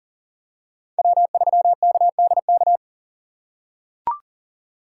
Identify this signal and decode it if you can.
Morse code, heard as a keyed tone, about 40 wpm, 700 Hz. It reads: W3KDK
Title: USA Call Signs - 5 Characters - 4x Character Spacing - 40wpm